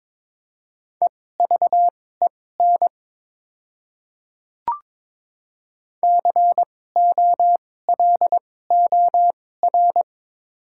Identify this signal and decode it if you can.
Morse code, heard as a keyed tone, about 22 wpm, spaced as for 16 wpm, 700 Hz. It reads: EVEN COLOR
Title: 2x Character Spacing - Top 300 Words - 22wpm